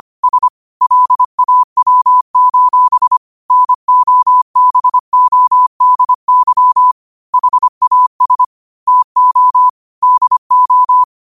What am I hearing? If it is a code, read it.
Morse code, heard as a keyed tone, about 25 wpm, 1000 Hz. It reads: S LAW: NOBODY HAS TO DO
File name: qotdfiles / 25wpm